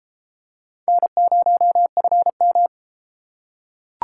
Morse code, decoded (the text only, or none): N0FM